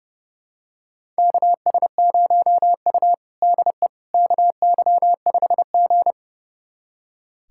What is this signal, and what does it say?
Morse code, heard as a keyed tone, about 29 wpm, 700 Hz. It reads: KS0U DE KY5G
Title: USA Callsign DE Callsign - 4 Characters - 30wpm